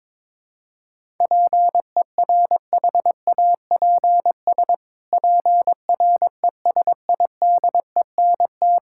Morse code, heard as a keyed tone, about 22 wpm, 700 Hz. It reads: PERHAPS PRESIDENT
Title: Pangram Challenge v2 - 22wpm